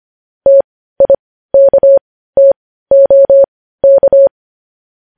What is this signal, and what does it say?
Morse code, heard as a keyed tone, about 25 wpm, spaced as for 15 wpm, 550 Hz. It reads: TIKTOK
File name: podcast / News headlines-1x-25x15